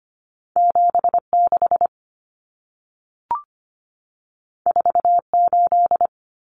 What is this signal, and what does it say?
Morse code, heard as a keyed tone, about 25 wpm, 700 Hz. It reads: 76 48
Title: Common Cut Numbers - 2-Digit 25wpm